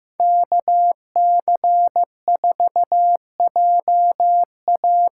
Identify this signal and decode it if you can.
Morse code, heard as a keyed tone, about 15 wpm, 700 Hz. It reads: KC4JA
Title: USA Call Signs - 6 Characters - 8x Character Spacing - 15wpm